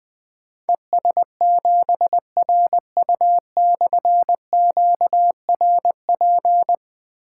Morse code, heard as a keyed tone, about 21 wpm, 700 Hz. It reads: ES7RU/QRP
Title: Difficult Call Signs 20wpm